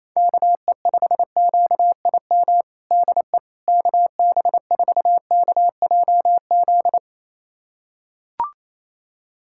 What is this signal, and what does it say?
Morse code, heard as a keyed tone, about 28 wpm, 700 Hz. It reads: KE5QIM DE KB4KJZ
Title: USA Callsign DE Callsign - 6 Characters - 28wpm